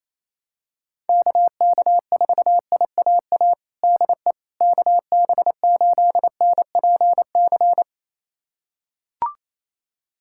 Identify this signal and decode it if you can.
Morse code, heard as a keyed tone, about 28 wpm, 700 Hz. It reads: KK4IAA DE KB8NPC
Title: USA Callsign DE Callsign - 6 Characters - 28wpm